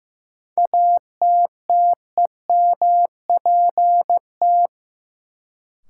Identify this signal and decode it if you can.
Morse code, heard as a keyed tone, about 15 wpm, 700 Hz. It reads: ATTEMPT
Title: Sets of 4 Words - New Word Drill - 15wpm